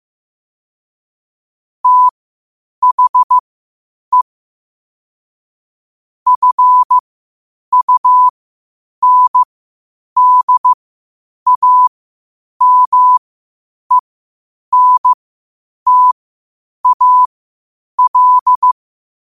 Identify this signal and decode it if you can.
Morse code, heard as a keyed tone, about 15 wpm, spaced as for 8 wpm, 1000 Hz. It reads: THE FUNDAMENTAL
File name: qotdfiles / 8wpm